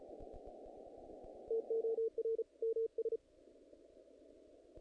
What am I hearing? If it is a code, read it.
Morse code, heard as a keyed tone, about 33 wpm, 470 Hz. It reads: TOR MS
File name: contests / naqp-cw-k5tr-op-k5pi- -1900